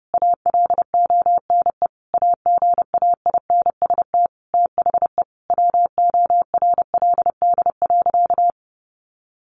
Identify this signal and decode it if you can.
Morse code, heard as a keyed tone, about 30 wpm, 700 Hz. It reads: ALONE AGAINST THE WORLD.
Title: Sentences from Top 1400 Words 30wpm